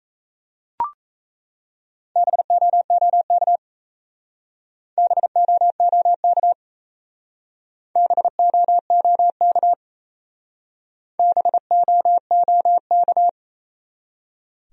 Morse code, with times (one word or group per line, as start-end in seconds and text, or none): 2.2-3.6 BOOK
5.0-6.5 BOOK
8.0-9.7 BOOK
11.2-13.3 BOOK